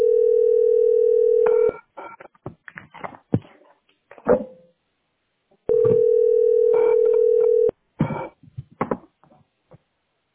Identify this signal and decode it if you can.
Ringback tone (North America), 440 + 480 Hz.